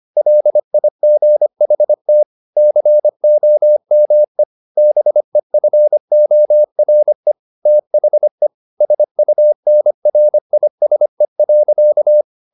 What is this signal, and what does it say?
Morse code, heard as a keyed tone, about 25 wpm, 600 Hz. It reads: LIGHT COME BEFORE THE SUNRISE.